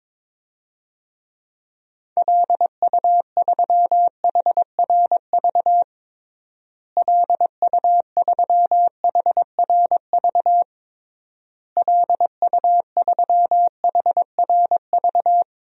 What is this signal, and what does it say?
Morse code, heard as a keyed tone, about 22 wpm, 700 Hz. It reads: LU3HRV LU3HRV LU3HRV
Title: DXCC-100 Call Signs - 6 Characters - Repeated 3x - 22wpm